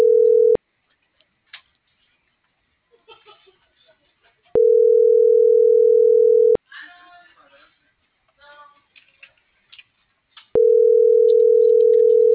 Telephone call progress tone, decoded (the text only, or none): ringback tone